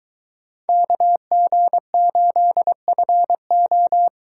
Morse code, heard as a keyed tone, about 21 wpm, 700 Hz. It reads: KG8FO